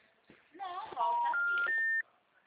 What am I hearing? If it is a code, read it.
Special information tone, with speech.
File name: out- -2020- - - .77659